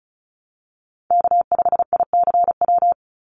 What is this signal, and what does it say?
Morse code, heard as a keyed tone, about 34 wpm, 700 Hz. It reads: K5ICW
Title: USA Call Signs 35wpm